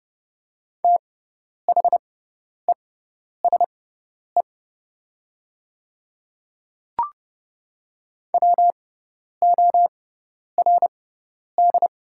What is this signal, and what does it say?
Morse code, heard as a keyed tone, about 31 wpm, spaced as for 10 wpm, 700 Hz. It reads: THESE WORD
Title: 6x Character Spacing - Top 100 Words - 30wpm